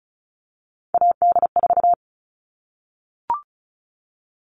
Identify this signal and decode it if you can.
Morse code, heard as a keyed tone, about 34 wpm, 700 Hz. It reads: AD4
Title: All Cut Numbers - 3-Digit 35wpm